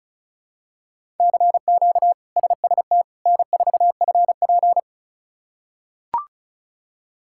Morse code, heard as a keyed tone, about 34 wpm, 700 Hz. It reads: CQ SST N4FP
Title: K1USN SST QSO Practice - 35wpm